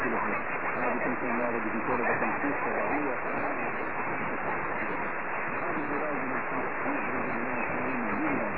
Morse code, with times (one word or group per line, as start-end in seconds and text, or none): none